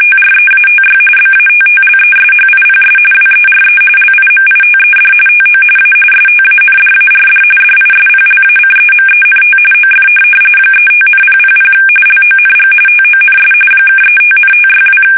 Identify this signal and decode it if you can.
RTTY, about 45.45 baud. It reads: TREATY BATTLESHIPS WERE LIMITED TO A STANDARD DISPLACEMENT OF 35,000 LONG TONS (35,562
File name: fsk850 / USS-Massachusetts